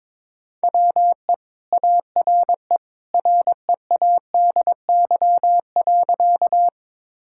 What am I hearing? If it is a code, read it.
Morse code, heard as a keyed tone, about 22 wpm, 700 Hz. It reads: WE ARE READY.
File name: practice / 22wpm / Sentences with 2-Word Phrases - 22wpm